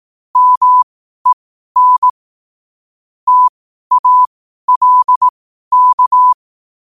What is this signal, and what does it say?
Morse code, heard as a keyed tone, about 18 wpm, spaced as for 12 wpm, 1000 Hz. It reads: MEN TALK
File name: qotdfiles / 12wpm